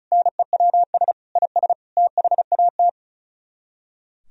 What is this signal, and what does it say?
Morse code, heard as a keyed tone, about 45 wpm, 700 Hz. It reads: NEWS IS THAT